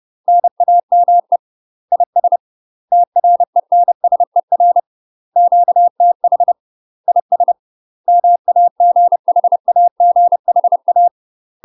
Morse code, heard as a keyed tone, about 30 wpm, 700 Hz. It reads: NAME IS TRENSER QTH IS MAGHAGHA
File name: UK MASTER QSOs TRAINING / 30WPM / UK MASTER QSOs 0040